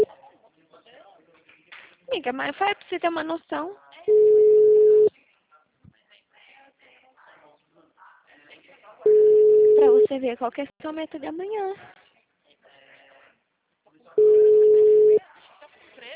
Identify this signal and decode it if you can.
Ringback tone (Europe, 425 Hz), with speech.